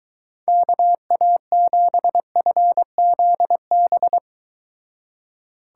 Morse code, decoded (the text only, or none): KA7FZB